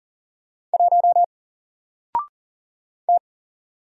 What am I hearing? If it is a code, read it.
Morse code, heard as a keyed tone, about 40 wpm, 700 Hz. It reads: 1 T
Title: Single Letter-Number 40wpm